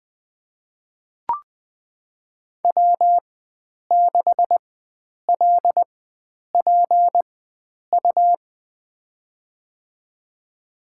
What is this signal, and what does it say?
Morse code, heard as a keyed tone, about 20 wpm, spaced as for 9 wpm, 700 Hz. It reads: W6LPU
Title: USA Call Signs - 5 Characters - 4x Character Spacing - 20wpm